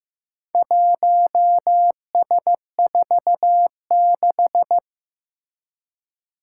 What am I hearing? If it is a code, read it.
Morse code, heard as a keyed tone, about 15 wpm, 700 Hz. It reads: 1S46
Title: Four Challenging Letter-Number Combinations 15wpm